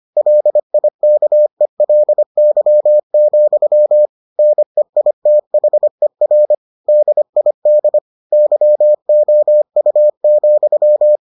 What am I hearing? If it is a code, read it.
Morse code, heard as a keyed tone, about 25 wpm, 600 Hz. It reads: LIKELY, NEITHER DID YOU,